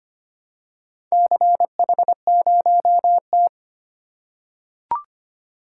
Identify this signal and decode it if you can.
Morse code, heard as a keyed tone, about 25 wpm, 700 Hz. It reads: CH0T
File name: practice / 25wpm / Four Letter-Number Combinations - 25wpm